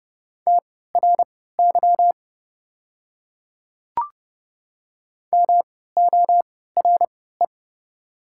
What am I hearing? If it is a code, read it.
Morse code, heard as a keyed tone, about 30 wpm, spaced as for 17 wpm, 690 Hz. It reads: TRY MORE
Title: 3x Character Spacing - Top 200 Words - 30wpm